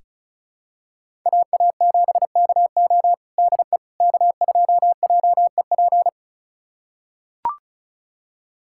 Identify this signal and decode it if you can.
Morse code, heard as a keyed tone, about 36 wpm, 700 Hz. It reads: AA7KO DE K2JEP
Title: USA Callsign DE Callsign - 5 Characters - 35wpm